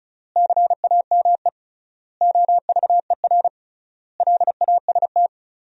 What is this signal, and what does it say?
Morse code, heard as a keyed tone, about 36 wpm, 700 Hz. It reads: CAME OVER LAST
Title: 3x Word-Spacing - Sentences from Top 300 Words - 35wpm